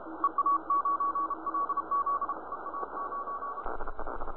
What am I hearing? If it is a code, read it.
Morse code, heard as a keyed tone, about 33 wpm, 1130 Hz, with noise. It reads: EA9LZ EA9
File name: ct1boh / flare